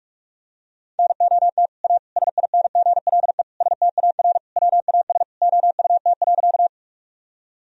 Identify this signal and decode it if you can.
Morse code, heard as a keyed tone, about 41 wpm, 700 Hz. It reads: NOT A SINGLE STAR WAS OUT.
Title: Sentences from Top 1000 Words - Review Part 4 - 45wpm